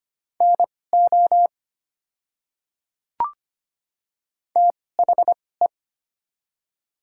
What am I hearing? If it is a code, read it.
Morse code, heard as a keyed tone, about 25 wpm, spaced as for 18 wpm, 700 Hz. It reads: NO THE
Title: 2x Character Spacing - Top 100 Words - 25wpm